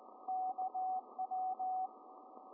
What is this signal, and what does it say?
Morse code, heard as a keyed tone, about 17 wpm, 750 Hz, with radio noise. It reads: KW